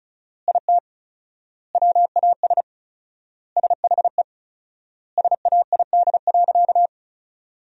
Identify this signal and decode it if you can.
Morse code, heard as a keyed tone, about 36 wpm, 700 Hz. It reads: IT WAS SHE SAID.
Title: 4x Word-Spacing - Sentences from Top 400 Words - 35wpm